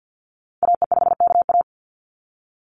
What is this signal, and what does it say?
Morse code, heard as a keyed tone, about 48 wpm, 700 Hz. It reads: AE5KA
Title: NA SOTA Activator Callsigns 50wpm